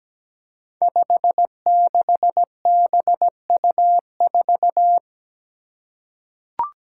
Morse code, heard as a keyed tone, about 17 wpm, 700 Hz. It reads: 56BU4
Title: Five Challenging Letter-Number Combinations 17wpm